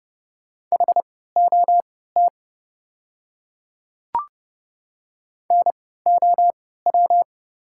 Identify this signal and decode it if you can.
Morse code, heard as a keyed tone, about 30 wpm, spaced as for 17 wpm, 700 Hz. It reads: HOT NOW
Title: 3x Character Spacing - Top 100 Words - 30wpm